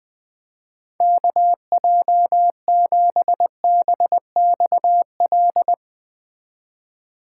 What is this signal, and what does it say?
Morse code, heard as a keyed tone, about 20 wpm, 700 Hz. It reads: KJ7BXL